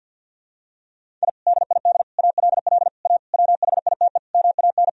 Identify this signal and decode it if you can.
Morse code, heard as a keyed tone, about 51 wpm, 700 Hz. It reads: I DID ALL A WHITE MAN
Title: Sentences from Top 1000 Words - Review Part 1 - 50wpm